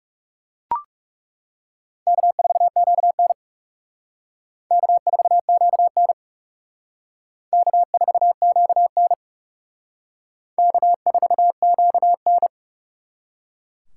Morse code, heard as a keyed tone, about 37 wpm, 700 Hz. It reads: K4QN K4QN K4QN K4QN
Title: Speed Racing - USA Call Signs 30wpm